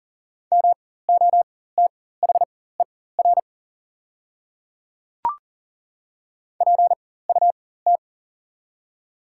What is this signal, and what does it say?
Morse code, heard as a keyed tone, about 40 wpm, spaced as for 19 wpm, 700 Hz. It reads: MOTHER PUT
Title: 4x Character Spacing - Top 300 Words - 40wpm